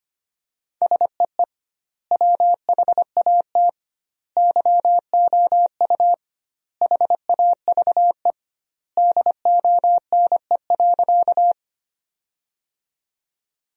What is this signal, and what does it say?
Morse code, heard as a keyed tone, about 25 wpm, 700 Hz. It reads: SEE WHAT YOU HAVE DONE.
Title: 2x Word-Spacing - Sentences from Top 500 Words - 25wpm